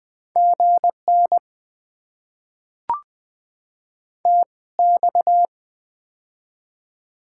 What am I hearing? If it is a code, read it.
Morse code, heard as a keyed tone, about 20 wpm, spaced as for 17 wpm, 700 Hz. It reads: GN TX
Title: Top 90 QSO Elements - 2x Character Spacing - 20wpm